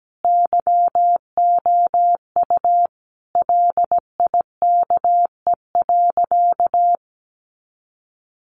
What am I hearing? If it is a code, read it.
Morse code, heard as a keyed tone, about 17 wpm, 700 Hz. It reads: YOU LIKE.
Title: Sentences from Top 1400 Words 17wpm